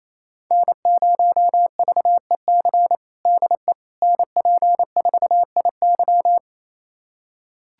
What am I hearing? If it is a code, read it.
Morse code, heard as a keyed tone, about 28 wpm, 700 Hz. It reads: N0VEC DE NP4IY
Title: USA Callsign DE Callsign - 5 Characters - 28wpm